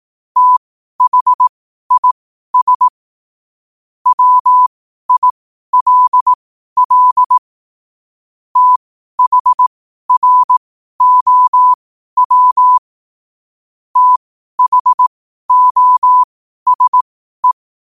Morse code, heard as a keyed tone, about 18 wpm, spaced as for 12 wpm, 1000 Hz. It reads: THIS WILL THROW THOSE
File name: qotdfiles / 12wpm